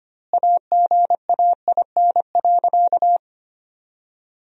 Morse code, heard as a keyed tone, about 25 wpm, 700 Hz. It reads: AGAIN.
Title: 2x Word-Spacing - Sentences from Top 200 Words - 25wpm